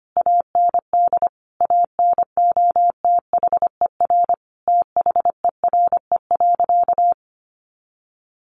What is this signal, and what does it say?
Morse code, heard as a keyed tone, about 25 wpm, 700 Hz. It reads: AND ANOTHER THERE.
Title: Sentences from Top 1400 Words 25wpm